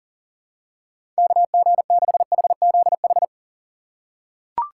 Morse code, heard as a keyed tone, about 37 wpm, 700 Hz. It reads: KG6HZH
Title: USA Call Signs - 6 Characters - 8x Character Spacing - 40wpm